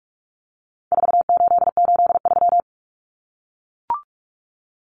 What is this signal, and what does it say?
Morse code, heard as a keyed tone, about 44 wpm, 700 Hz. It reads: 4883
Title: Numbers - 4-Digit 45wpm